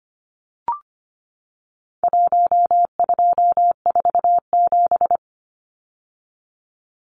Morse code, heard as a keyed tone, about 25 wpm, 700 Hz. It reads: 1247
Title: Numbers - Contest Style - 4-Digit 25wpm